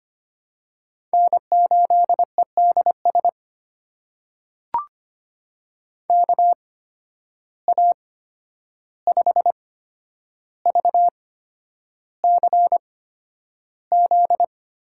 Morse code, heard as a keyed tone, about 25 wpm, spaced as for 10 wpm, 700 Hz. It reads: N8EDS KA5VCZ
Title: USA Call Signs - 8x Character Spacing - 25wpm